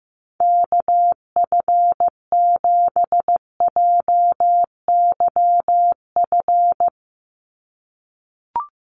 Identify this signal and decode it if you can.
Morse code, heard as a keyed tone, about 15 wpm, 700 Hz. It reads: KF7JYF
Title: USA Call Signs 15wpm